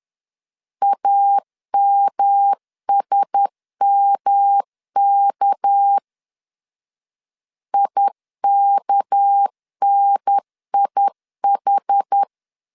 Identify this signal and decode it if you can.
Morse code, heard as a keyed tone, about 10 wpm, 800 Hz. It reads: AMSMK IKNIH
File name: CW / DET / 1 6 1500C S B 10wpm 4137